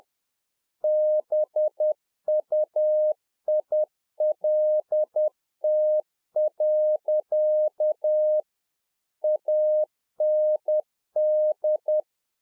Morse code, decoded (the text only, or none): BUILT. AND